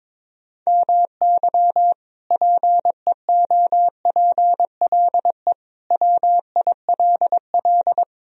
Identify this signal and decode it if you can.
Morse code, heard as a keyed tone, about 22 wpm, 700 Hz. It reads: MY PEOPLE WILL
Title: Sentences from Top 1000 Words - Review Part 2 - 22wpm